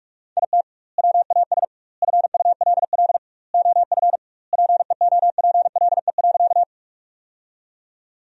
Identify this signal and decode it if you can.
Morse code, heard as a keyed tone, about 46 wpm, 700 Hz. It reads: IT WAS FULL OF PEOPLE.
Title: 2x Word-Spacing - Sentences from Top 500 Words - 45wpm